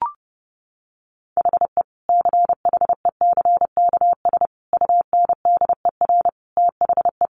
Morse code, heard as a keyed tone, about 30 wpm, 700 Hz. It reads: HE CHECKS UNDER THE
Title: Easy Phrases 30wpm